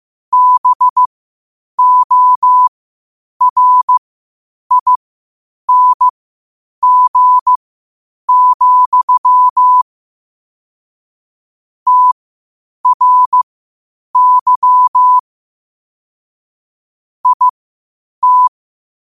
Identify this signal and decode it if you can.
Morse code, heard as a keyed tone, about 15 wpm, spaced as for 8 wpm, 1000 Hz. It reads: BORING, TRY IT
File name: qotdfiles / 8wpm